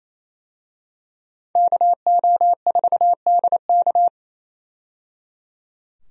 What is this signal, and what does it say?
Morse code, heard as a keyed tone, about 27 wpm, 700 Hz. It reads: KO4DK
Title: USA Call Signs - 5 Characters - 28wpm